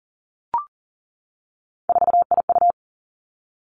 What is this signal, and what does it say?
Morse code, heard as a keyed tone, about 37 wpm, 700 Hz. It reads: 4IU
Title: Three Challenging Letter-Number Combinations 40wpm